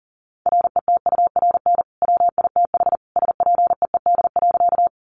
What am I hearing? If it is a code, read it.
Morse code, heard as a keyed tone, about 40 wpm, 700 Hz. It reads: RETURN WITH SPEED.
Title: Sentences from Top 800 Words 40wpm